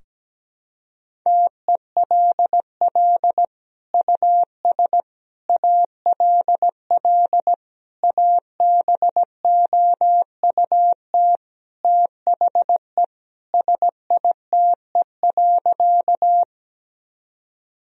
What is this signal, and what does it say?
Morse code, heard as a keyed tone, about 17 wpm, 700 Hz. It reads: TELL US ALL ABOUT THE SITE.
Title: Sentences from Top 1000 Words - Review Part 4 - 17wpm